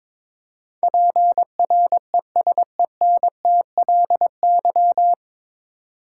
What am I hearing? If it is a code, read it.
Morse code, heard as a keyed tone, about 22 wpm, 690 Hz. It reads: PRESENTLY
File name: 9 Letter Words - First 3 Letters Given - 22wpm